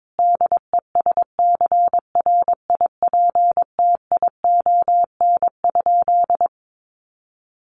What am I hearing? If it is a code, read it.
Morse code, heard as a keyed tone, about 22 wpm, 700 Hz. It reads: DESCRIPTION?
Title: Sentences from Top 1500 Words 22wpm